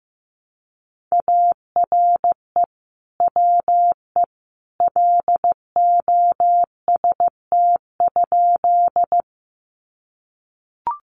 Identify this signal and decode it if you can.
Morse code, heard as a keyed tone, about 15 wpm, 700 Hz. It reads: ARE WE LOST?
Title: Sentences from Top 1400 Words 15wpm